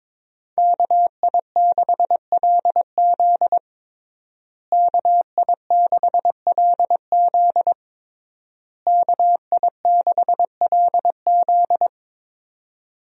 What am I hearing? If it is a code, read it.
Morse code, heard as a keyed tone, about 23 wpm, 700 Hz. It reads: KI6LZ KI6LZ KI6LZ
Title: USA Call Signs - 5 Characters - Repeated 3x - 22wpm